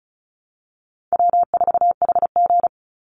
Morse code, heard as a keyed tone, about 34 wpm, 700 Hz. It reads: W4HG
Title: Easy Call Signs Repeated 2x - 35wpm